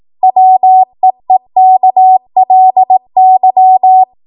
Morse code, heard as a keyed tone, about 18 wpm, 750 Hz. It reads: WEEKLY
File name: Morse / 18WPM